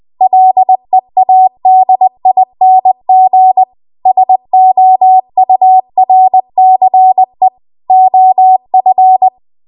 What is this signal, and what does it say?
Morse code, heard as a keyed tone, about 20 wpm, 750 Hz. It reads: LEADING SOURCE OF
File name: Archive / 20 WPM / 20WPM